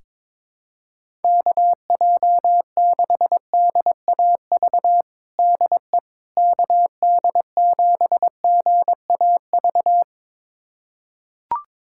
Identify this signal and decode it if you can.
Morse code, heard as a keyed tone, about 22 wpm, 700 Hz. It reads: KJ6DAV DE KD7GAV
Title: USA Callsign DE Callsign - 6 Characters - 22wpm